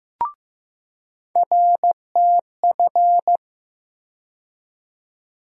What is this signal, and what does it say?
Morse code, heard as a keyed tone, about 15 wpm, 700 Hz. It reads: RTF
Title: USA Call Sign Postfixes - 3 Letters 15wpm